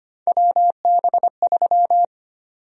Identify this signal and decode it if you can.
Morse code, heard as a keyed tone, about 25 wpm, 700 Hz. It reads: WB3
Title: USA Call Sign Prefixes - 25wpm